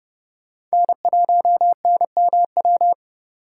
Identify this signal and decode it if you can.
Morse code, heard as a keyed tone, about 37 wpm, 700 Hz. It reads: N1NMW
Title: Speed Racing - USA 1x3 Call Signs 30wpm